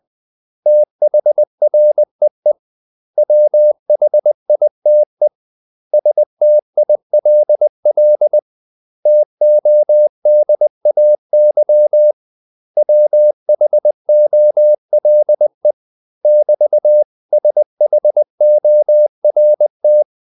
Morse code, heard as a keyed tone, about 20 wpm, 600 Hz. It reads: THREE WHITE STILL TODAY WHOLE <BT>SHORT
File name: CW / Practice / 500MostCommonEnglishWords20